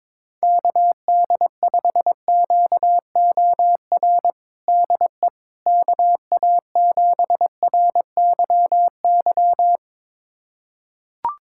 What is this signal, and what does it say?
Morse code, heard as a keyed tone, about 22 wpm, 700 Hz. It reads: KD5QOR DE KA7RYY
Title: USA Callsign DE Callsign - 6 Characters - 22wpm